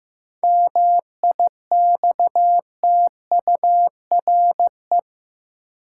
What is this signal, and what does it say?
Morse code, heard as a keyed tone, about 15 wpm, 700 Hz. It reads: MIXTURE
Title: Seven Letter Words - 15wpm